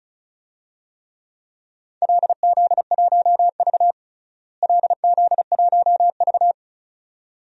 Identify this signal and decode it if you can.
Morse code, heard as a keyed tone, about 36 wpm, 700 Hz. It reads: LZ1V LZ1V
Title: DXCC-100 Call Signs - 4 Characters - Repeated 2x - 35wpm